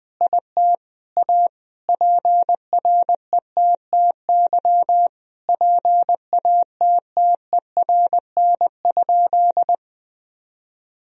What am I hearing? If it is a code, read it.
Morse code, heard as a keyed tone, about 21 wpm, 700 Hz. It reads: IT A PRETTY PATTERN?
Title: Sentences from Top 1000 Words - Review Part 2 - 20wpm